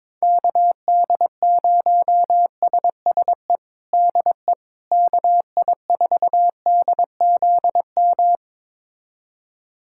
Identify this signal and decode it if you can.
Morse code, heard as a keyed tone, about 22 wpm, 700 Hz. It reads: KD0SSE DE KI4DZM
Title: USA Callsign DE Callsign 22wpm